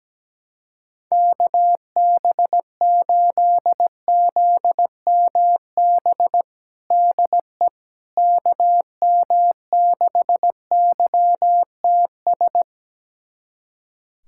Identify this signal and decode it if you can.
Morse code, heard as a keyed tone, about 17 wpm, 700 Hz. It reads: KB8ZMB DE KM6YTS
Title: USA Callsign DE Callsign - 6 Characters - 17wpm